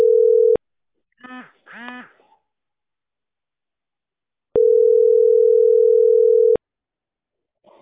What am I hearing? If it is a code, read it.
Ringback tone (North America), 440 + 480 Hz.